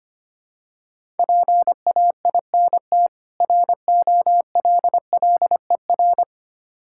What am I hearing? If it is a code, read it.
Morse code, heard as a keyed tone, about 25 wpm, 700 Hz. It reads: PAINT ROLLER
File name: practice / 25wpm / Difficult Words with Hints - 25wpm